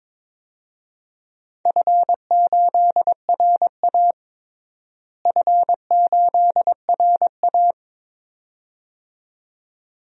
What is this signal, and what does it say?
Morse code, heard as a keyed tone, about 22 wpm, 700 Hz. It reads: F8RA F8RA
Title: DXCC-100 Call Signs - 4 Characters - Repeated 2x - 22wpm